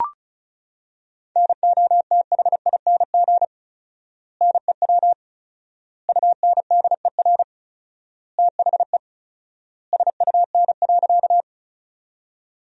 Morse code, heard as a keyed tone, about 36 wpm, 700 Hz. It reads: NOTHING NEW UNDER THE SUN.
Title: 4x Word-Spacing - Sentences from Top 500 Words - 35wpm